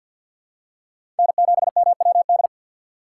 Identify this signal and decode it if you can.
Morse code, heard as a keyed tone, about 53 wpm, 700 Hz. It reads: N7MWD
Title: USA Call Signs - 6x Character Spacing - 50wpm